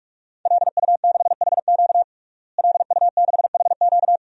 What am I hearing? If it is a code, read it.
Morse code, heard as a keyed tone, about 42 wpm, 700 Hz. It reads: LU6HQ LU6HQ
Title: DXCC-100 Call Signs - 5 Characters - Repeated 2x - 45wpm